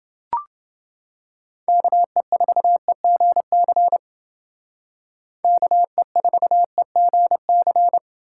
Morse code, heard as a keyed tone, about 29 wpm, 700 Hz. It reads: KE4EGC KE4EGC